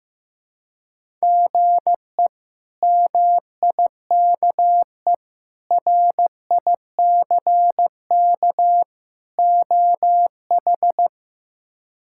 Morse code, heard as a keyed tone, about 15 wpm, 700 Hz. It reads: GE MIKE RICK OH
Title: K1USN SST QSO Practice - 15wpm